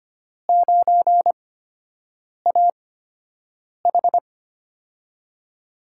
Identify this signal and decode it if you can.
Morse code, heard as a keyed tone, about 25 wpm, spaced as for 7 wpm, 700 Hz. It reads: 9AH
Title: USA Call Signs - 8x Character Spacing - 25wpm